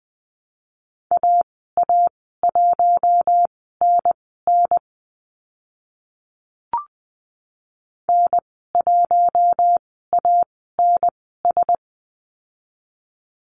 Morse code, heard as a keyed tone, about 20 wpm, spaced as for 14 wpm, 700 Hz. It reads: AA1NN N1ANS
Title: 042 Lesson 8g - Number 1 - Call Signs 20wpm